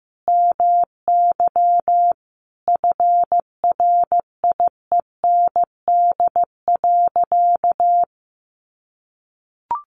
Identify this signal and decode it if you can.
Morse code, heard as a keyed tone, about 15 wpm, 700 Hz. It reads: MY FRIEND.
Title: Sentences from Top 500 Words 15wpm